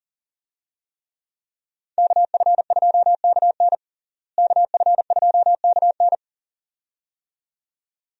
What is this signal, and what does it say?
Morse code, heard as a keyed tone, about 37 wpm, 700 Hz. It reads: KF2KN KF2KN
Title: USA Call Signs - 5 Characters - Repeated 2x - 40wpm